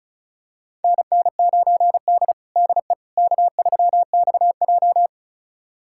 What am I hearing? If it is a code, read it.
Morse code, heard as a keyed tone, about 36 wpm, 700 Hz. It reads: NN9D DE K3XJ
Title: USA Callsign DE Callsign - 4 Characters - 35wpm